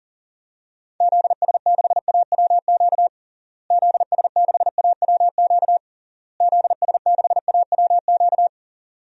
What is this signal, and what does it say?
Morse code, heard as a keyed tone, about 40 wpm, 700 Hz. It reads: ZS6AWQ ZS6AWQ ZS6AWQ
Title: DXCC-100 Call Signs - 6 Characters - Repeated 3x - 40wpm